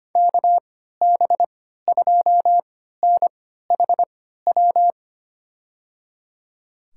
Morse code, heard as a keyed tone, about 25 wpm, spaced as for 14 wpm, 700 Hz. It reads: KB2NHW
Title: USA Call Signs - 6 Characters - 3x Character Spacing - 25wpm